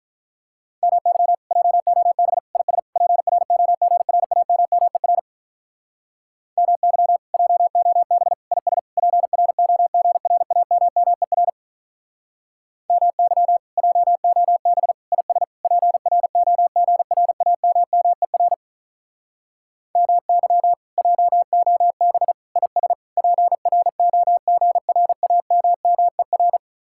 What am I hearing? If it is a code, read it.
Morse code, heard as a keyed tone, about 44 wpm, 700 Hz. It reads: MY JOB IS PROGRAMMER MY JOB IS PROGRAMMER MY JOB IS PROGRAMMER MY JOB IS PROGRAMMER
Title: Speed-Racing Common QSO Phrases Encore - 35wpm